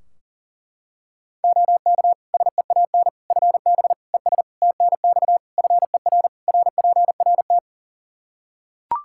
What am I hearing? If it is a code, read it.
Morse code, heard as a keyed tone, about 40 wpm, 700 Hz. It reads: OK SEAN FB ES TNX FER RPRT
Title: Practice QSO - Exchange 2 - Info - 40wpm